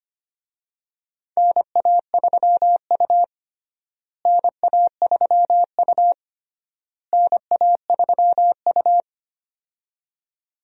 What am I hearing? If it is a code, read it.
Morse code, heard as a keyed tone, about 25 wpm, 700 Hz. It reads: NA3U NA3U NA3U
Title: USA Call Signs - 4 Characters - Repeated 3x - 25wpm